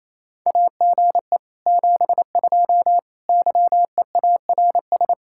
Morse code, heard as a keyed tone, about 34 wpm, 700 Hz. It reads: AGE 72 YEARS
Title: Speed-Racing Common QSO Phrases - 28wpm